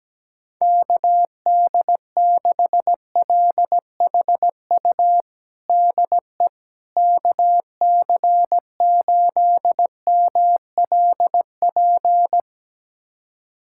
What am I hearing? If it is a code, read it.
Morse code, heard as a keyed tone, about 17 wpm, 700 Hz. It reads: KD6LHU DE KC8MLP
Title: USA Callsign DE Callsign 17wpm